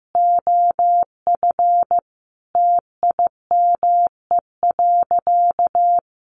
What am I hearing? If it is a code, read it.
Morse code, heard as a keyed tone, about 15 wpm, 700 Hz. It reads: OF TIME.
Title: A Sound Of Thunder 15wpm